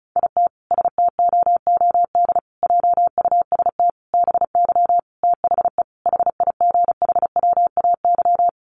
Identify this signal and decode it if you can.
Morse code, heard as a keyed tone, about 34 wpm, 700 Hz. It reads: IT STOOD JUST BY THE HIGHWAY
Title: Sentences from Top 1400 Words 35wpm